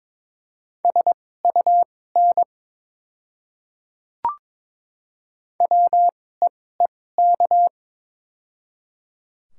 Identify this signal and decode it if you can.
Morse code, heard as a keyed tone, about 22 wpm, spaced as for 16 wpm, 700 Hz. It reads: SUN WEEK
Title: 2x Character Spacing - Top 500 Words - 22wpm